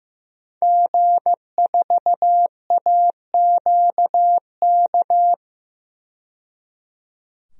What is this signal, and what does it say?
Morse code, heard as a keyed tone, about 15 wpm, 700 Hz. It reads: G4AQK
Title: DXCC-100 Call Signs - 5 Characters - 15wpm